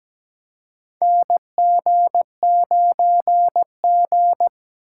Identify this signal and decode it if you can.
Morse code, heard as a keyed tone, about 17 wpm, 700 Hz. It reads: NG9G
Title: USA Call Signs - 4 Characters - 2x Character Spacing - 17wpm